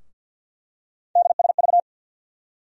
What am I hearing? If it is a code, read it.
Morse code, heard as a keyed tone, about 53 wpm, 700 Hz. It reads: DSV
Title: USA Call Sign Postfixes - 3 Letters 50wpm